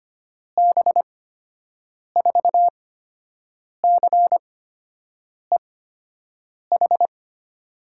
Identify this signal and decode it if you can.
Morse code, heard as a keyed tone, about 25 wpm, spaced as for 7 wpm, 700 Hz. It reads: B4CEH
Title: USA Call Signs - 6 Characters - 8x Character Spacing - 25wpm